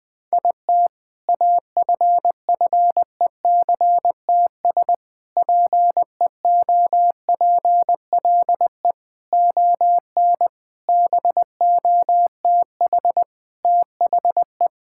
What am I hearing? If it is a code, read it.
Morse code, heard as a keyed tone, about 20 wpm, 700 Hz. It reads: IT AFFECTS PEOPLE ON BOTH THE
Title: Sentences with 7-Word Phrases - 20wpm